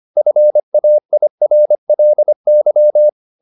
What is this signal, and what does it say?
Morse code, heard as a keyed tone, about 25 wpm, 600 Hz. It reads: FAIRLY